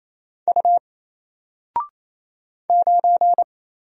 Morse code, heard as a keyed tone, about 29 wpm, 700 Hz. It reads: U 9
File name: Single-Letter-Number-28wpm